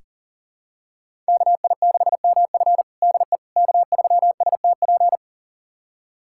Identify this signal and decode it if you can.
Morse code, heard as a keyed tone, about 40 wpm, 700 Hz. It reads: KI6MF DE K3STP
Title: USA Callsign DE Callsign - 5 Characters - 40wpm